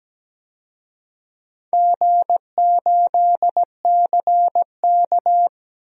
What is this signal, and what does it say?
Morse code, heard as a keyed tone, about 17 wpm, 700 Hz. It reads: G8CK
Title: DXCC-100 Call Signs - 4 Characters - Repeated 3x - 17wpm